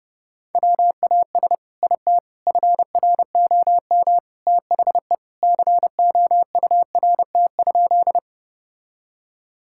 Morse code, heard as a keyed tone, about 32 wpm, 700 Hz. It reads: WAS IT FROM THE COURT?
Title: Sentences from Top 1600 Words 30wpm